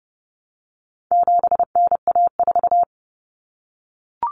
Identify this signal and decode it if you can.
Morse code, heard as a keyed tone, about 30 wpm, 700 Hz. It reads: 7NA4
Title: Common Cut Numbers - 4-Digit 30wpm